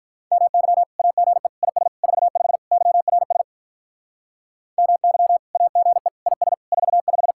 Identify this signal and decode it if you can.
Morse code, heard as a keyed tone, about 43 wpm, 700 Hz. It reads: MY AGE IS 45 YRS MY AGE IS 45
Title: Speed-Racing Common QSO Phrases - 35wpm